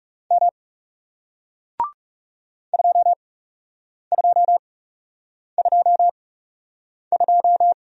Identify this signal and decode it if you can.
Morse code, heard as a keyed tone, about 40 wpm, spaced as for 8 wpm, 700 Hz. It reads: M 2222